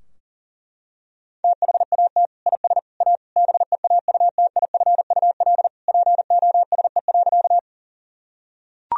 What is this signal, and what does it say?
Morse code, heard as a keyed tone, about 40 wpm, 700 Hz. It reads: THAT IS A BEAUTIFUL POSE.
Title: Sentences from Top 1000 Words - Review Part 2 - 40wpm